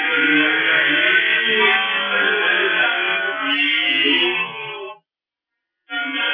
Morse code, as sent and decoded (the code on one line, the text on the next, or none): none